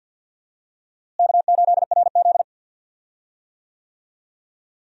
Morse code, heard as a keyed tone, about 51 wpm, 700 Hz. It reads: K8RZ
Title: USA Call Signs - 4 Characters - Repeated 2x - 50wpm